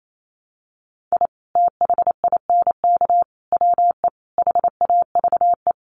Morse code, heard as a keyed tone, about 28 wpm, 700 Hz. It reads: I THINK WE HAVE
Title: Sets of 4 Words 28wpm